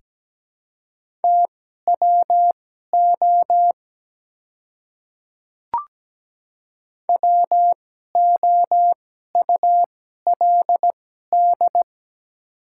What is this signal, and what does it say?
Morse code, heard as a keyed tone, about 17 wpm, spaced as for 12 wpm, 700 Hz. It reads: TWO WOULD
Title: 2x Character Spacing - Top 100 Words - 17wpm